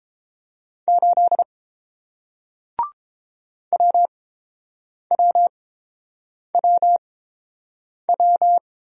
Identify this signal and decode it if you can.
Morse code, heard as a keyed tone, about 29 wpm, spaced as for 7 wpm, 700 Hz. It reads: 8 WWWW